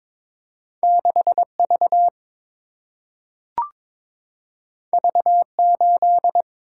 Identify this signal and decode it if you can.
Morse code, heard as a keyed tone, about 22 wpm, 700 Hz. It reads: 6 V V 8
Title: Single-Letter Single-Number 22wpm